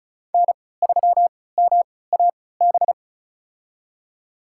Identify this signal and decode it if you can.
Morse code, heard as a keyed tone, about 36 wpm, spaced as for 20 wpm, 700 Hz. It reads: N3MAB
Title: USA Call Signs - 5 Characters - 3x Character Spacing - 35wpm